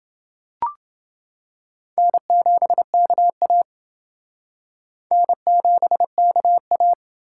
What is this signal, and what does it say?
Morse code, heard as a keyed tone, about 28 wpm, 700 Hz. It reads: N7KA N7KA